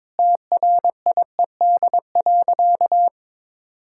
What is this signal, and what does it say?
Morse code, heard as a keyed tone, about 22 wpm, 700 Hz. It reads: TRIED.
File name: practice / 22wpm / Sentences with 8-Word Phrases - 22wpm